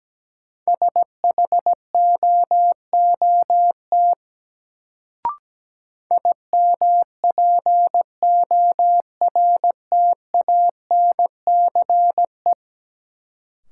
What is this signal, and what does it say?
Morse code, heard as a keyed tone, about 17 wpm, 700 Hz. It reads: SHOOT IMPORTANCE
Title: Sentences from Top 1200 Words - New Word Drill - 17wpm